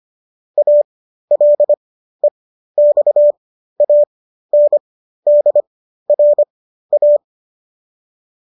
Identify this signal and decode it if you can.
Morse code, heard as a keyed tone, about 25 wpm, spaced as for 13 wpm, 590 Hz. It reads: ALEXANDRA